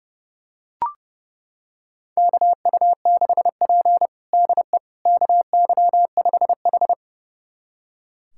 Morse code, heard as a keyed tone, about 32 wpm, 700 Hz. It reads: KU6P DE KY5H
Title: USA Callsign DE Callsign - 4 Characters - 30wpm